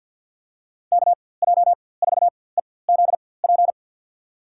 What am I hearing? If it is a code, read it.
Morse code, heard as a keyed tone, about 46 wpm, spaced as for 23 wpm, 700 Hz. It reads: KJ4EZP
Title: USA Call Signs - 6 Characters - 4x Character Spacing - 50wpm